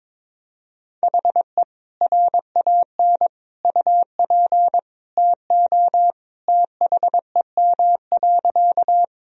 Morse code, heard as a keyed tone, about 22 wpm, 700 Hz. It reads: HE RAN UP TO THEM.
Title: Sentences from Top 1000 Words - Review Part 3 - 22wpm